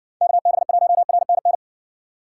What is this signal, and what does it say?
Morse code, heard as a keyed tone, about 60 wpm, 700 Hz. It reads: KB1RNN